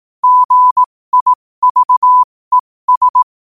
Morse code, heard as a keyed tone, about 18 wpm, spaced as for 15 wpm, 1000 Hz. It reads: GIVES
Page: qotd-at-15-wpm